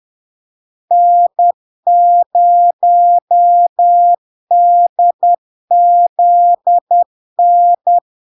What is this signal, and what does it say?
Morse code, heard as a keyed tone, about 10 wpm, 700 Hz. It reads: N0DZN
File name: LARK-FD-2022-Log-in-CW-10wpm-700-Hz